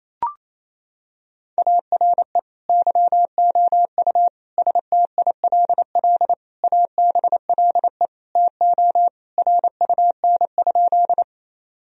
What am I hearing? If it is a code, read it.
Morse code, heard as a keyed tone, about 27 wpm, 700 Hz. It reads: ARE YOU STILL ABLE TO RUN?
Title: Sentences from Top 1000 Words - Review Part 2 - 28wpm